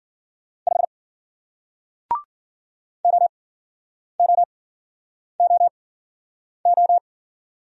Morse code, heard as a keyed tone, about 51 wpm, spaced as for 9 wpm, 700 Hz. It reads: 5 OOOO